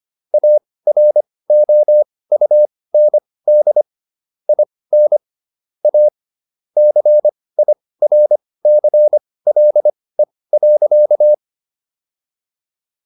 Morse code, heard as a keyed tone, about 25 wpm, spaced as for 18 wpm, 600 Hz. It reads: AROUND IN A CIRCLE.
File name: SS123 18